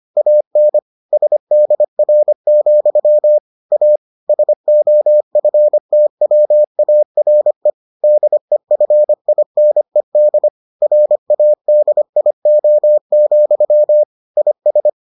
Morse code, heard as a keyed tone, about 25 wpm, 600 Hz. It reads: AN SDR, A SOFTWARE DEFINED RADIO, IS